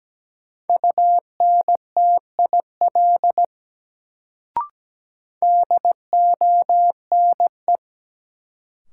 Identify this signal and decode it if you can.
Morse code, heard as a keyed tone, about 17 wpm, 700 Hz. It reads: UNTIL DONE
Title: Sets of 5 Words Encore - New Word Drill - 17wpm